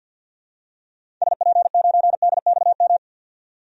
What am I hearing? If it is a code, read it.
Morse code, heard as a keyed tone, about 45 wpm, 700 Hz. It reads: SP9DXM
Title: DXCC-100 Call Signs - 6 Characters - 50wpm